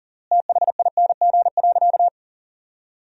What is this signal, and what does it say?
Morse code, heard as a keyed tone, about 40 wpm, 700 Hz. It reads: THING.
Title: 3x Word-Spacing - Sentences from Top 400 Words - 40wpm